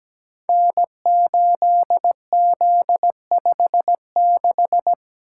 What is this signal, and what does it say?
Morse code, heard as a keyed tone, about 17 wpm, 700 Hz. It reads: N8Z56
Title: Code Groups - 5 Letter:Numbers - 17wpm